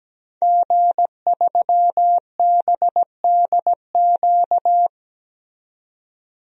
Morse code, heard as a keyed tone, about 17 wpm, 700 Hz. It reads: G3BDQ
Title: DXCC-100 Call Signs - 5 Characters - 17wpm